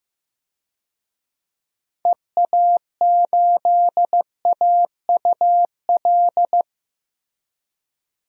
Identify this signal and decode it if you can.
Morse code, heard as a keyed tone, about 15 wpm, 700 Hz. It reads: EA8AUL
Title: DXCC-100 Call Signs - 6 Characters - Repeated 3x - 15wpm